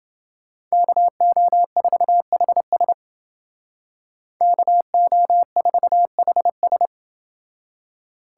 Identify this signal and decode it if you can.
Morse code, heard as a keyed tone, about 28 wpm, 700 Hz. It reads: KO4HS KO4HS